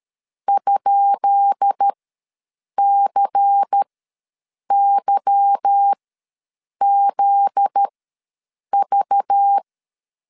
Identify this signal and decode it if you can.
Morse code, heard as a keyed tone, about 12 wpm, spaced as for 7 wpm, 800 Hz. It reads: ?CYZV
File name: LEZIONI / 10 SIGNS / 10C DOT ws 12wpm 3548